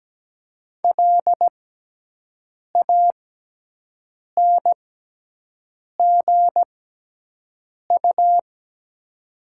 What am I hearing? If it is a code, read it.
Morse code, heard as a keyed tone, about 17 wpm, spaced as for 6 wpm, 700 Hz. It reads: LANGU
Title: 6x Character Spacing - Top 500 Words - 17wpm